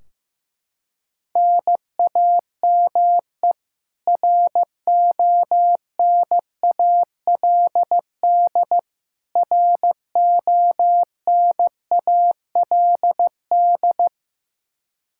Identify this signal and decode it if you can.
Morse code, heard as a keyed tone, about 15 wpm, 700 Hz. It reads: NAME RONALD RONALD
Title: Practice QSO - Exchange 1 - Beginning - 15wpm